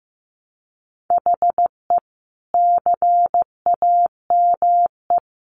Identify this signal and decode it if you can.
Morse code, heard as a keyed tone, about 15 wpm, 700 Hz. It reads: HE CAME
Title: Sentences from Top 1300 Words 15wpm